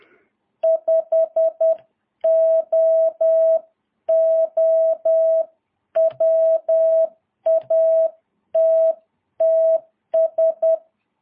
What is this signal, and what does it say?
Morse code, heard as a keyed tone, about 10 wpm, 650 Hz. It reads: 5OO WATTS
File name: jh8cbh / 2022cw